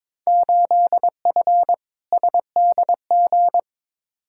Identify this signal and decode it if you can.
Morse code, heard as a keyed tone, about 22 wpm, 700 Hz. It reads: 8F SDG
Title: Field Day Exchanges Encore - 22wpm